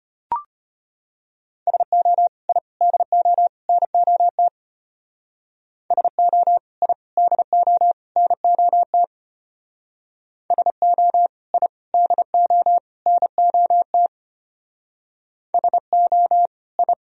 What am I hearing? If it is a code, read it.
Morse code, heard as a keyed tone, about 31 wpm, 700 Hz. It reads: SO I DO NOT SO I DO NOT SO I DO NOT SO I